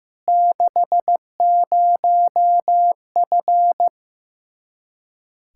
Morse code, heard as a keyed tone, about 15 wpm, 700 Hz. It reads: 60F
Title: Practice QSO - Exchange 2 - Info - 15wpm